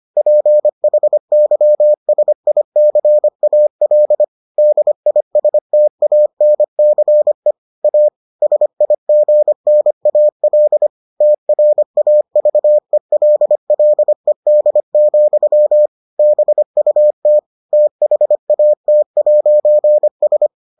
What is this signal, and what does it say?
Morse code, heard as a keyed tone, about 25 wpm, 600 Hz. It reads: PHYSICAL DISTANCE A SIGNAL TRAVELLED, BUT THAT'S